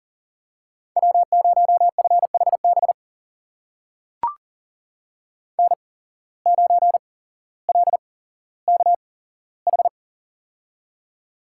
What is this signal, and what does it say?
Morse code, heard as a keyed tone, about 42 wpm, spaced as for 17 wpm, 700 Hz. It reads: W0FHB N9LKH
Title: USA Call Signs - 5 Characters - 8x Character Spacing - 40wpm